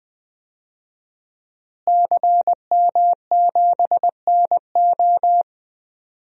Morse code, heard as a keyed tone, about 20 wpm, 700 Hz. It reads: CM7NO
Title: DXCC-100 Call Signs - 5 Characters - Repeated 3x - 20wpm